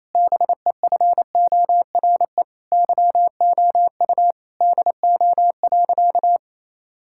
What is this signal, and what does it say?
Morse code, heard as a keyed tone, about 28 wpm, 700 Hz. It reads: BEFORE YOU DO.
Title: Sentences from Top 1000 Words - Review Part 6 - 28wpm